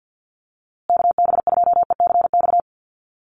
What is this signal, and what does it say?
Morse code, heard as a keyed tone, about 50 wpm, 700 Hz. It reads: KB2ECX
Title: USA Call Signs Repeated 3x 50wpm